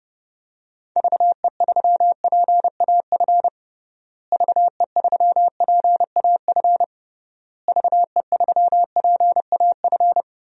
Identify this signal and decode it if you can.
Morse code, heard as a keyed tone, about 30 wpm, 700 Hz. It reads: VE3PAF VE3PAF VE3PAF
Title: DXCC-100 Call Signs - 6 Characters - Repeated 3x - 30wpm